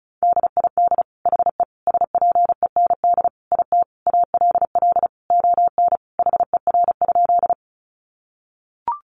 Morse code, heard as a keyed tone, about 34 wpm, 700 Hz. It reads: DID HE SPEND IT ALL ON HER?
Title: Sentences from Top 800 Words 35wpm